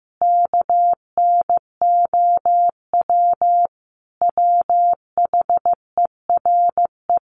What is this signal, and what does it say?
Morse code, heard as a keyed tone, about 15 wpm, 700 Hz. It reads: KNOW WHERE
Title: Sentences from Top 600 Words 15wpm